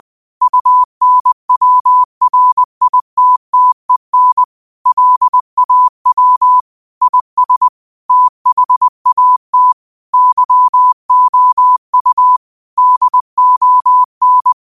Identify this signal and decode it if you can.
Morse code, heard as a keyed tone, about 20 wpm, 1000 Hz. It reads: UNWRITTEN LAW IS THAT YOU DON
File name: qotdfiles / 20wpm